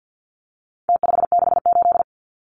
Morse code, heard as a keyed tone, about 48 wpm, 700 Hz. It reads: T568
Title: Numbers - Contest Style - 4-Digit 50wpm